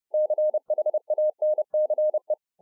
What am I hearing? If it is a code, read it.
Morse code, heard as a keyed tone, about 30 wpm, 610 Hz. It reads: CHANCE